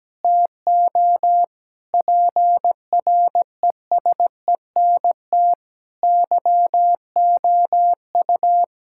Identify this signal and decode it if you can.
Morse code, heard as a keyed tone, about 17 wpm, 700 Hz. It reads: TO PRESENT YOU
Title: Sentences from Top 1000 Words - Review Part 6 - 17wpm